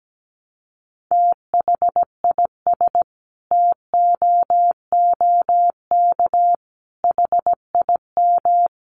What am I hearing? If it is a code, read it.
Morse code, heard as a keyed tone, about 17 wpm, 700 Hz. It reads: THIS TOOK HIM
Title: Sentences from Top 300 Words 17wpm